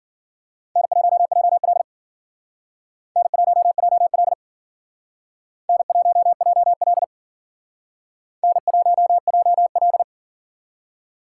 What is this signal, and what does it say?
Morse code, heard as a keyed tone, about 51 wpm, 700 Hz. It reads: N1JL N1JL N1JL N1JL